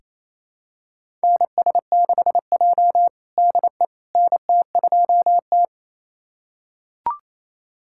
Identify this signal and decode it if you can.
Morse code, heard as a keyed tone, about 28 wpm, 700 Hz. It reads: NS6J DE NT2T
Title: USA Callsign DE Callsign - 4 Characters - 28wpm